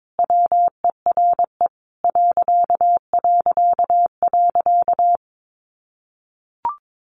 Morse code, heard as a keyed tone, about 22 wpm, 700 Hz. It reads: WERE ...
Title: A Sound Of Thunder 22wpm